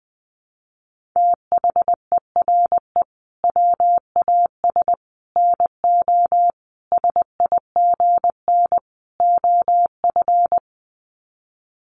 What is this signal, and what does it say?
Morse code, heard as a keyed tone, about 20 wpm, 700 Hz. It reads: THERE WAS NO SIGN OF
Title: Sets of 5 Words Encore2 20wpm